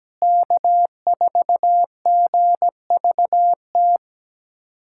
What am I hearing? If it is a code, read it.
Morse code, heard as a keyed tone, about 17 wpm, 700 Hz. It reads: K4GVT
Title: USA Callsign DE Callsign - 5 Characters - 17wpm